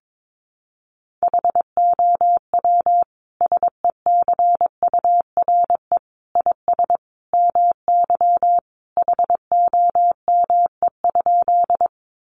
Morse code, heard as a keyed tone, about 22 wpm, 700 Hz. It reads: HOW SECURE IS MY HOME?
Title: Sentences from Top 1200 Words 22wpm